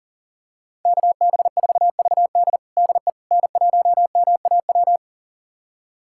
Morse code, heard as a keyed tone, about 40 wpm, 700 Hz. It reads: KB4VD DE N1MAW
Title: USA Callsign DE Callsign - 5 Characters - 40wpm